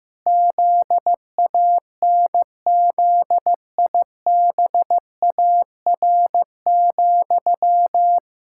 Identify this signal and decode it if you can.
Morse code, heard as a keyed tone, about 15 wpm, 700 Hz. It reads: ZANZIBAR,
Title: Pangram Challenge v2 - 15wpm